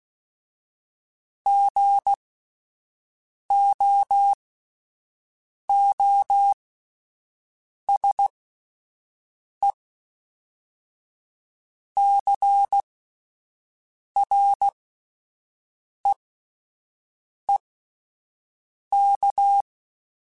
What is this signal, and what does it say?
Morse code, heard as a keyed tone, about 16 wpm, spaced as for 5 wpm, 800 Hz. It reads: GOOSECREEK